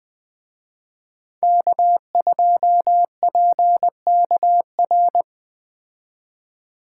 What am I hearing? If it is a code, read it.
Morse code, heard as a keyed tone, about 20 wpm, 700 Hz. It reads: K2PKR